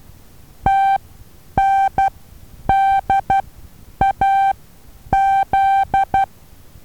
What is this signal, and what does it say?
Morse code, heard as a keyed tone, about 12 wpm, spaced as for 9 wpm, 780 Hz. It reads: TNDAZ
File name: Taviro HA6VA B